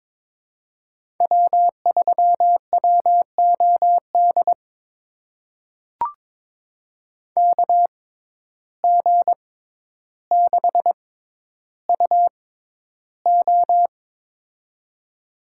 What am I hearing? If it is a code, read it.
Morse code, heard as a keyed tone, about 22 wpm, spaced as for 11 wpm, 700 Hz. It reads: W3WOD KG6UO
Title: USA Call Signs - 5 Characters - 6x Character Spacing - 22wpm